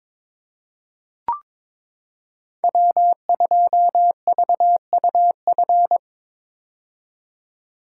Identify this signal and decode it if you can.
Morse code, heard as a keyed tone, about 22 wpm, 700 Hz. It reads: W2VUF
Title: K1USN SST QSO Practice - 22wpm